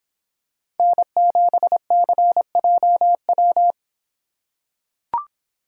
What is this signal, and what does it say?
Morse code, heard as a keyed tone, about 24 wpm, 700 Hz. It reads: N7CJW